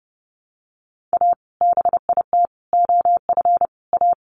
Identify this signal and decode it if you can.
Morse code, heard as a keyed tone, about 30 wpm, 700 Hz. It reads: A BIT OF A
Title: Sets of 4 Words 30wpm